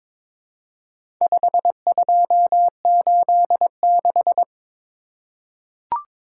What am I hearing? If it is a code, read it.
Morse code, heard as a keyed tone, about 22 wpm, 700 Hz. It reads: 5286
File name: practice / 22wpm / Four Letter-Number Combinations - 22wpm